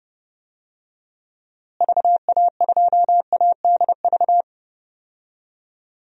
Morse code, heard as a keyed tone, about 30 wpm, 700 Hz. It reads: VA2ADV
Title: DXCC-100 Call Signs - 6 Characters - Repeated 3x - 30wpm